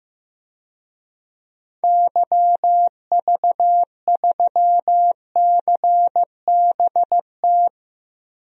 Morse code, heard as a keyed tone, about 15 wpm, 700 Hz. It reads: YV3CBT
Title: DXCC-100 Call Signs - 6 Characters - Repeated 3x - 15wpm